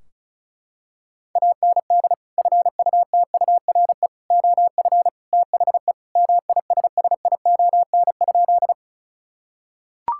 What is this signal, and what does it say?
Morse code, heard as a keyed tone, about 34 wpm, 700 Hz. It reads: AND FUTURE OF THE MISSION?
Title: Sentences from Top 1600 Words 35wpm